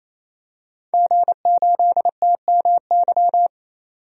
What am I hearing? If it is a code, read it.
Morse code, heard as a keyed tone, about 28 wpm, 700 Hz. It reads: G8TMY
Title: DXCC-100 Call Signs - 5 Characters - 28wpm